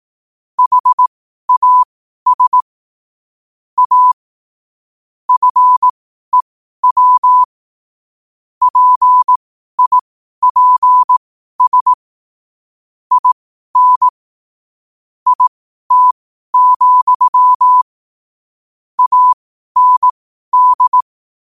Morse code, heard as a keyed tone, about 18 wpm, spaced as for 12 wpm, 1000 Hz. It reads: HAS A FEW PIPS IN IT, AND
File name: qotdfiles / 12wpm